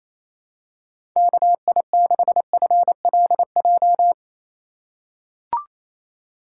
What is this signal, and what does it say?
Morse code, heard as a keyed tone, about 28 wpm, 700 Hz. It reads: KI6FLJ
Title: USA Call Signs - 6 Characters - 3x Character Spacing - 28wpm